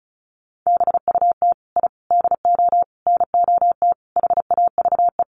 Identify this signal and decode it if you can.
Morse code, heard as a keyed tone, about 34 wpm, 700 Hz. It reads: BUT I DO NOT HAVE
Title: Sets of 5 Words Encore 35wpm